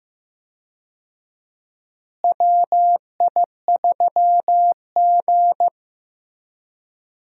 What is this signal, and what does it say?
Morse code, heard as a keyed tone, about 15 wpm, 700 Hz. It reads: WI3G
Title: USA Call Signs - 4 Characters - Repeated 3x - 15wpm